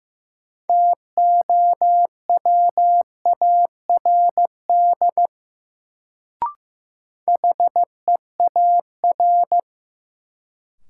Morse code, heard as a keyed tone, about 16 wpm, 700 Hz. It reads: TOWARD HEAR
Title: Sets of 5 Words - New Word Drill - 15wpm